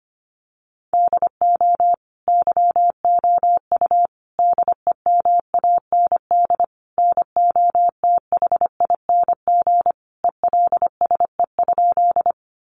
Morse code, heard as a keyed tone, about 25 wpm, 700 Hz. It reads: DO YOU DEMAND NOTHING ELSE?
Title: Sentences from Top 800 Words 25wpm